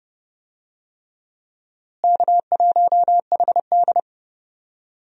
Morse code, heard as a keyed tone, about 30 wpm, 700 Hz. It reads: K1HD
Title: USA Call Signs - 4 Characters - Repeated 3x - 30wpm